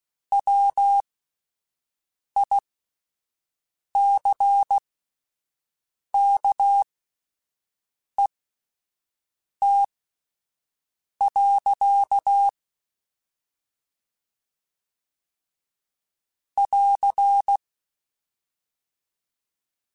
Morse code, heard as a keyed tone, about 16 wpm, spaced as for 5 wpm, 800 Hz. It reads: WICKET. <AR>
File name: Misc / Misc-mp3s / Word Pairs @5